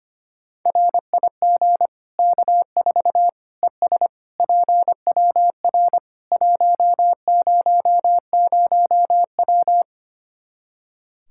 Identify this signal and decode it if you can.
Morse code, heard as a keyed tone, about 25 wpm, 700 Hz. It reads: RIG K4 ES PWR 100W
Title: Practice QSO - Exchange 2 - Info - 25wpm